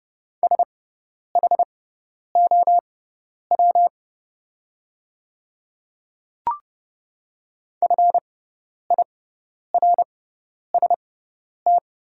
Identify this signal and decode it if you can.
Morse code, heard as a keyed tone, about 30 wpm, spaced as for 10 wpm, 700 Hz. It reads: SHOW FIRST
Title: 6x Character Spacing - Top 200 Words - 30wpm